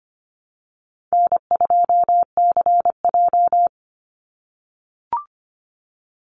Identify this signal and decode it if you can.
Morse code, heard as a keyed tone, about 25 wpm, 700 Hz. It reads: N2CJ
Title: Call Signs - 25wpm